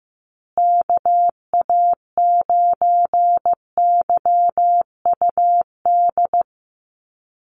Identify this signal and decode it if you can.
Morse code, heard as a keyed tone, about 15 wpm, 700 Hz. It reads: KA9YUD
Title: USA Call Signs Repeated 3x 15wpm